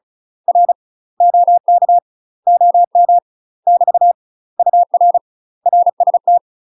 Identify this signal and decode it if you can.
Morse code, heard as a keyed tone, about 34 wpm, 700 Hz. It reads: R OK OM <BT> UR RST
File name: UK MASTER QSOs TRAINING / 35WPM / UK MASTER QSOs 0005